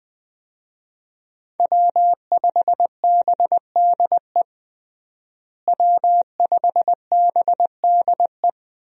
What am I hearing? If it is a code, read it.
Morse code, heard as a keyed tone, about 20 wpm, 700 Hz. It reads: W5BDE W5BDE
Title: USA Call Signs - 5 Characters - Repeated 2x - 20wpm